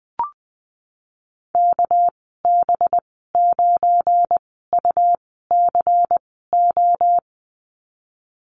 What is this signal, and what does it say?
Morse code, heard as a keyed tone, about 20 wpm, spaced as for 15 wpm, 700 Hz. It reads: KB9UCO
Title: 200 Lesson 32g - Letter k - Call Signs 20wpm